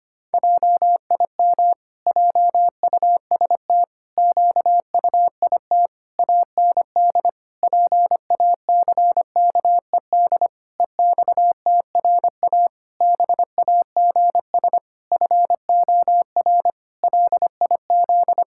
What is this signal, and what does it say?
Morse code, heard as a keyed tone, about 25 wpm, 700 Hz. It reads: JIM JUST QUIT AND PACKED EXTRA BAGS FOR LIZ
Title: Pangram Challenge v2 - 25wpm